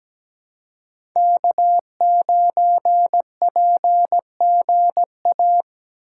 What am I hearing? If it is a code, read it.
Morse code, heard as a keyed tone, about 17 wpm, 700 Hz. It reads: K9PGA
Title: USA Call Signs - 5 Characters - 2x Character Spacing - 17wpm